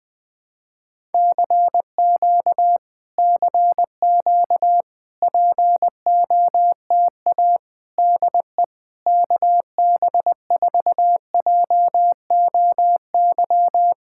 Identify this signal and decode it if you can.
Morse code, heard as a keyed tone, about 20 wpm, 700 Hz. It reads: CQ CQ POTA DE KB4JOY
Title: POTA QSO Practice - 20wpm